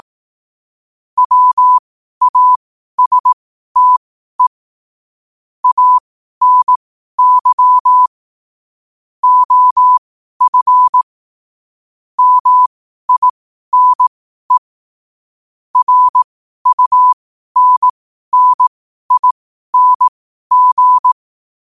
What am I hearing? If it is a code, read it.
Morse code, heard as a keyed tone, about 18 wpm, spaced as for 12 wpm, 1000 Hz. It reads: WASTE ANY OF MINE RUNNING
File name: qotdfiles / 12wpm